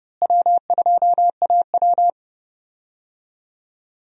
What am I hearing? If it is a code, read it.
Morse code, heard as a keyed tone, about 30 wpm, 700 Hz. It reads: W2AW
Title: USA Call Signs - 4 Characters - Repeated 3x - 30wpm